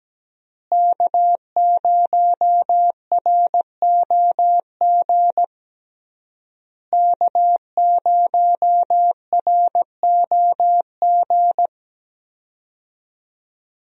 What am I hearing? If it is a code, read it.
Morse code, heard as a keyed tone, about 17 wpm, 700 Hz. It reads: K0ROG K0ROG
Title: USA Call Signs - 5 Characters - Repeated 3x - 17wpm